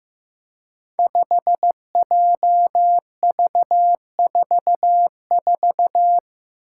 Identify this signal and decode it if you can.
Morse code, heard as a keyed tone, about 15 wpm, 700 Hz. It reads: 5JV44
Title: Five Challenging Letter-Number Combinations 15wpm